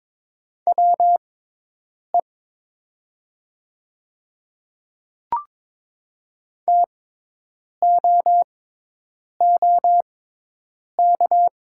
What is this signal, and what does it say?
Morse code, heard as a keyed tone, about 22 wpm, spaced as for 8 wpm, 700 Hz. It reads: WE TOOK
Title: 6x Character Spacing - Top 300 Words - 22wpm